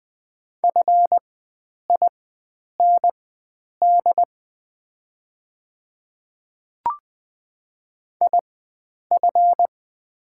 Morse code, heard as a keyed tone, about 20 wpm, spaced as for 9 wpm, 690 Hz. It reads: FIND IF
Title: 4x Character Spacing - Top 100 Words - 20wpm